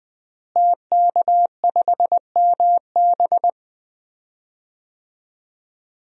Morse code, heard as a keyed tone, about 20 wpm, 700 Hz. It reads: TK5MB
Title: DXCC-100 Call Signs - 5 Characters - Repeated 3x - 20wpm